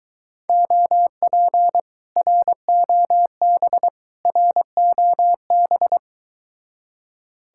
Morse code, heard as a keyed tone, about 23 wpm, 700 Hz. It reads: OP ROB ROB